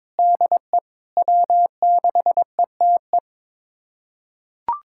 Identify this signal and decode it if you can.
Morse code, heard as a keyed tone, about 22 wpm, 700 Hz. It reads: DE W6ETE
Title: USA Callsign DE Callsign 22wpm